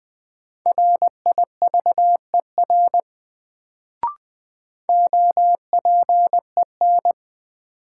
Morse code, heard as a keyed tone, about 20 wpm, 700 Hz. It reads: RIVER OPEN
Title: Sentences from Top 300 Words - New Word Drill - 20wpm